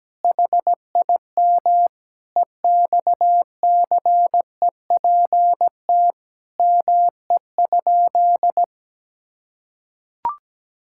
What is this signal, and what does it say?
Morse code, heard as a keyed tone, about 17 wpm, 700 Hz. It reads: HIM EXCEPT ME?
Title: Sentences from Top 1600 Words 17wpm